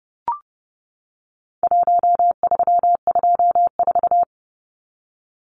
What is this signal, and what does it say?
Morse code, heard as a keyed tone, about 30 wpm, 700 Hz. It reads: 1324
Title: Numbers - Contest Style - 4-Digit 30wpm